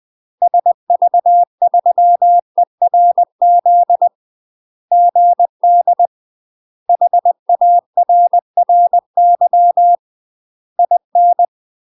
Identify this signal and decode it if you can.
Morse code, heard as a keyed tone, about 20 wpm, 700 Hz. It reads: SV3ERZ GD HARRY IN
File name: UK MASTER QSOs TRAINING / 20WPM / UK MASTER QSOs 0007